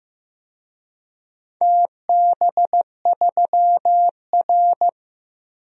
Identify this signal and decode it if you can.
Morse code, heard as a keyed tone, about 15 wpm, 700 Hz. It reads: TB3R
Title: DXCC-100 Call Signs - 4 Characters - Repeated 3x - 15wpm